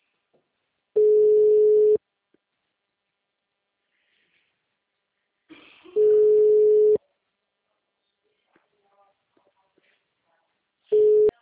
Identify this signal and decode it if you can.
Ringback tone (Europe, 425 Hz).
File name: out- -2011- - - .255